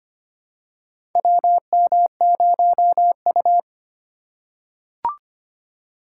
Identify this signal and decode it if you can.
Morse code, heard as a keyed tone, about 25 wpm, 700 Hz. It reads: WM0U
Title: USA Call Signs - 4 Characters - 4x Character Spacing - 25wpm